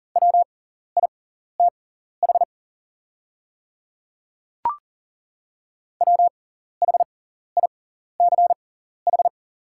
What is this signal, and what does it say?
Morse code, heard as a keyed tone, about 40 wpm, spaced as for 14 wpm, 690 Hz. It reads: WITH WHICH
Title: 6x Character Spacing - Top 100 Words - 40wpm